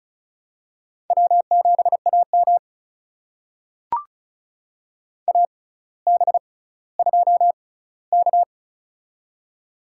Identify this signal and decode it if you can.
Morse code, heard as a keyed tone, about 36 wpm, spaced as for 18 wpm, 700 Hz. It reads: W7AM AB2K
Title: USA Call Signs - 4 Characters - 6x Character Spacing - 35wpm